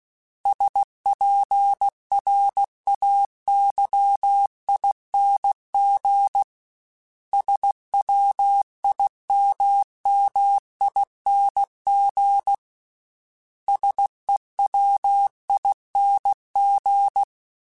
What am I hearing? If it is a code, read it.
Morse code, heard as a keyed tone, about 16 wpm, 800 Hz. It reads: SPRAYING SWIMMING SEWING
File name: Special INGs / Special INGs-mp3s / Sngl ing @15